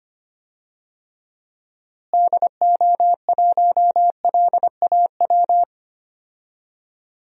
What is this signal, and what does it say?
Morse code, heard as a keyed tone, about 25 wpm, 700 Hz. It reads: DO1LAW
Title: DXCC-100 Call Signs - 6 Characters - Repeated 2x - 25wpm